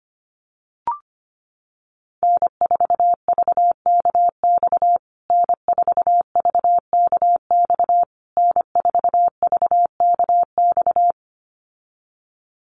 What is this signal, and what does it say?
Morse code, heard as a keyed tone, about 25 wpm, 700 Hz. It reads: N4VKX N4VKX N4VKX
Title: USA Call Signs Repeated 3x 25wpm